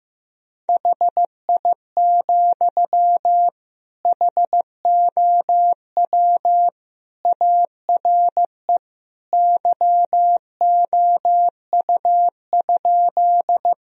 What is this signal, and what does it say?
Morse code, heard as a keyed tone, about 15 wpm, 700 Hz. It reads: HI, HOW ARE YOU?
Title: Sentences from Top 1000 Words - Review Part 3 - 15wpm